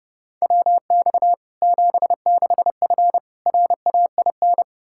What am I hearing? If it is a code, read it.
Morse code, heard as a keyed tone, about 32 wpm, 700 Hz. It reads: WX 76F RAIN
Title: Practice QSO - Exchange 2 - Info - 30wpm